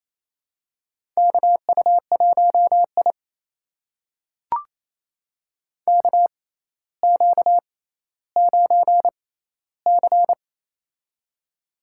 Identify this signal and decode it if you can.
Morse code, heard as a keyed tone, about 28 wpm, spaced as for 14 wpm, 700 Hz. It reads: KU1I KQ9C
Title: USA Call Signs - 4 Characters - 6x Character Spacing - 28wpm